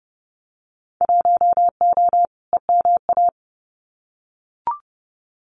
Morse code, heard as a keyed tone, about 30 wpm, 700 Hz. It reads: 1O EMA
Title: Winter Field Day Exchanges Encore 30wpm